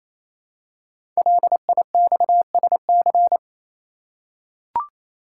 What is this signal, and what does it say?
Morse code, heard as a keyed tone, about 29 wpm, 700 Hz. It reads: LIXSC
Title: Code Groups - 5 Letters - 28wpm